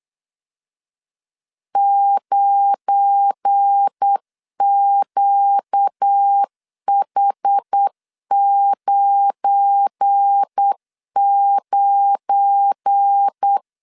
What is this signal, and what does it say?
Morse code, heard as a keyed tone, about 8 wpm, 800 Hz. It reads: 9QH99
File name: LEZIONI / 7 GF9Z / 7 GF9 8wpm 3545